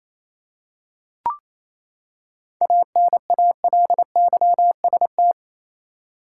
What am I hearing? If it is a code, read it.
Morse code, heard as a keyed tone, about 28 wpm, 700 Hz. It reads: ANALYST
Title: Practice QSO - Exchange 3 - More Info - 28wpm